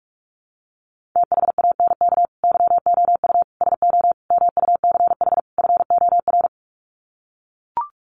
Morse code, heard as a keyed tone, about 44 wpm, 700 Hz. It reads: THANK YOU SO MUCH FOR
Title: Sets of 5 Words 45wpm